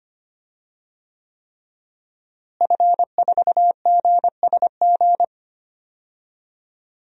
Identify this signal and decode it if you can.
Morse code, heard as a keyed tone, about 25 wpm, 700 Hz. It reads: F4GSG
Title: DXCC-100 Call Signs - 5 Characters - Repeated 3x - 25wpm